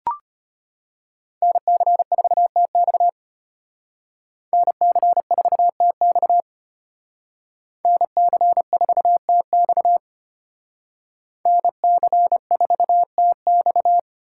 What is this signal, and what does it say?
Morse code, heard as a keyed tone, about 32 wpm, 700 Hz. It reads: NC4TX NC4TX NC4TX NC4TX